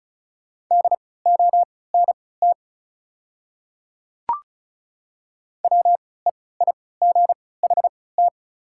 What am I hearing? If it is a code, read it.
Morse code, heard as a keyed tone, about 34 wpm, spaced as for 20 wpm, 690 Hz. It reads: DONT WEIGHT
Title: 3x Character Spacing - Top 500 Words - 35wpm